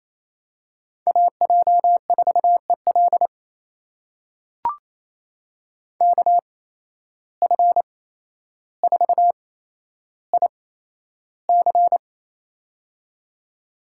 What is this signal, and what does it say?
Morse code, heard as a keyed tone, about 28 wpm, spaced as for 12 wpm, 700 Hz. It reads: AJ4EL KF4IC
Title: USA Call Signs - 5 Characters - 8x Character Spacing - 28wpm